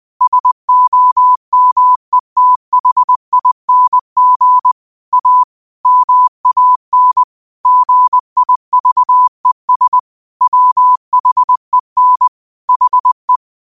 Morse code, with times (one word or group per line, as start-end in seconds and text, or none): 0.2-4.7 SOMETHING
5.1-5.4 A
5.9-7.2 MAN
7.7-10.0 GIVES
10.4-12.3 WHEN
12.7-13.4 HE